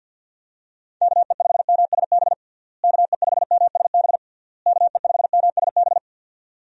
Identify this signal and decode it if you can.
Morse code, heard as a keyed tone, about 51 wpm, 700 Hz. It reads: KE5MSB KE5MSB KE5MSB
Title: USA Call Signs - 6 Characters - Repeated 3x - 50wpm